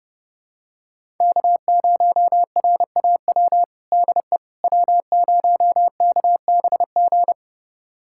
Morse code, heard as a keyed tone, about 30 wpm, 700 Hz. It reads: K0RAW DE W0KBG
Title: USA Callsign DE Callsign - 5 Characters - 30wpm